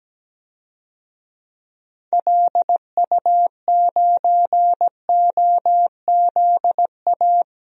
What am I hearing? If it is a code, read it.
Morse code, heard as a keyed tone, about 17 wpm, 700 Hz. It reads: LU9OZA
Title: DXCC-100 Call Signs - 6 Characters - 17wpm